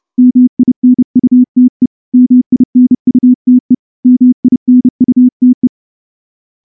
Morse code, heard as a keyed tone, about 28 wpm, 250 Hz. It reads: MINUTE MINUTE MINUTE
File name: morse-training-30WPM